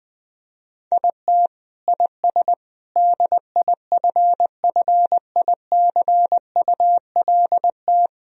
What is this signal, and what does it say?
Morse code, heard as a keyed tone, about 20 wpm, 700 Hz. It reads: IT IS DIFFICULT
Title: Sentences from Top 1000 Words - Review Part 1 - 20wpm